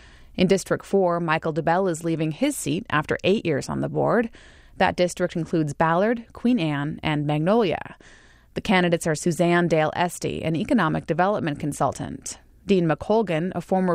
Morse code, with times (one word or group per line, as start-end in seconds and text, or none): none